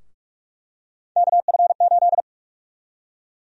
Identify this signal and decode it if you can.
Morse code, heard as a keyed tone, about 46 wpm, 700 Hz. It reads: KF8
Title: USA Call Sign Prefixes - Repeated 2x - 45wpm